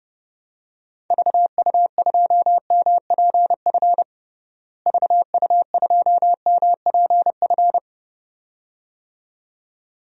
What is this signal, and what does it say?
Morse code, heard as a keyed tone, about 30 wpm, 700 Hz. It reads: VU2MPF VU2MPF
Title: DXCC-100 Call Signs - 6 Characters - Repeated 2x - 30wpm